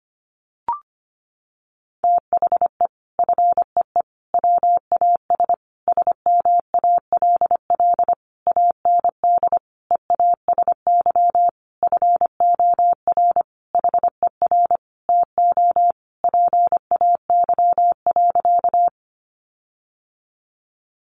Morse code, heard as a keyed tone, about 25 wpm, 700 Hz. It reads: THE FEE WAS SMALL AND EASY FOR HER TO PAY.
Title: Sentences from Top 1500 Words 25wpm